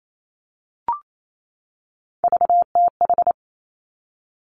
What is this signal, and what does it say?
Morse code, heard as a keyed tone, about 28 wpm, 700 Hz. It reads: VTH
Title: Three Challenging Letter-Number Combinations 28wpm